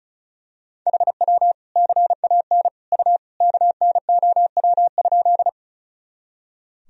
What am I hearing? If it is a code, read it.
Morse code, heard as a keyed tone, about 36 wpm, 700 Hz. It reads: HW CAN U KNOW?
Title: Sentences from Top 100 Words and Abbreviations 35wpm